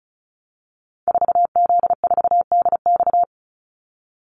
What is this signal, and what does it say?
Morse code, heard as a keyed tone, about 34 wpm, 700 Hz. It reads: 4Z4DX
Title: Call Signs - 35wpm